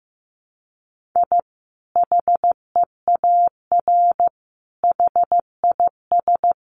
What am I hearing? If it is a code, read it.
Morse code, heard as a keyed tone, about 15 wpm, 700 Hz. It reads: I HEAR HIS
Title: Sentences from Top 500 Words 15wpm